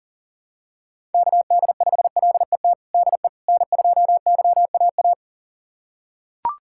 Morse code, heard as a keyed tone, about 37 wpm, 700 Hz. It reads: KD5LET DE N2YAA
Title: USA Callsign DE Callsign 40wpm